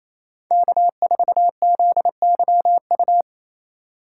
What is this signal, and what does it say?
Morse code, heard as a keyed tone, about 28 wpm, 700 Hz. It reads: K4ZYU
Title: USA Call Signs - 5 Characters - 4x Character Spacing - 28wpm